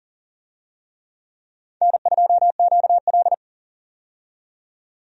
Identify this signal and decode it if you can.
Morse code, heard as a keyed tone, about 37 wpm, 700 Hz. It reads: N2QL
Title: USA Call Signs - 4 Characters - 40wpm